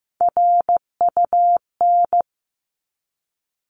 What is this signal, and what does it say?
Morse code, heard as a keyed tone, about 15 wpm, 700 Hz. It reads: RUN
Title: Top 300 Words 15wpm